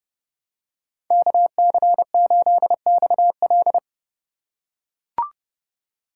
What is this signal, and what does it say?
Morse code, heard as a keyed tone, about 30 wpm, 700 Hz. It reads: KC8XL
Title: Code Groups - 5 Letter:Numbers - 30wpm